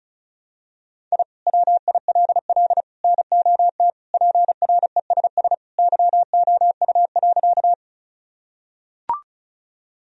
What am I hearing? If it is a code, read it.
Morse code, heard as a keyed tone, about 36 wpm, 700 Hz. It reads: I WILL NOT PRESS YOU.
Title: Sentences from Top 1000 Words - Review Part 1 - 35wpm